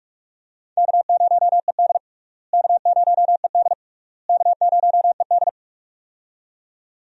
Morse code, heard as a keyed tone, about 41 wpm, 700 Hz. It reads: K0ED K0ED K0ED
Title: USA Call Signs - 4 Characters - Repeated 3x - 45wpm